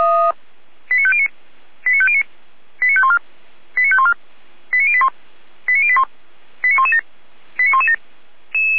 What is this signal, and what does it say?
Five-tone selective call (ZVEI), heard as quick runs of five tones, tones of about 80 ms. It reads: 89509 89509 89514 89514 89081 89081 89117 89117
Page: SDR/ZVEI